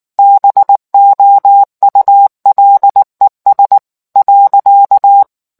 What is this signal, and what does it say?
Morse code, heard as a keyed tone, about 19 wpm, 800 Hz. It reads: BOULES.